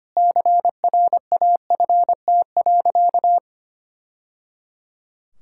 Morse code, heard as a keyed tone, about 25 wpm, 700 Hz. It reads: CRAFT.
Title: Sentences from Top 1000 Words - Review Part 6 - 25wpm